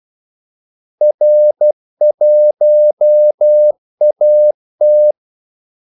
Morse code, heard as a keyed tone, about 12 wpm, 600 Hz. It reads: R1AT
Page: Kaldesignaler 12 wpm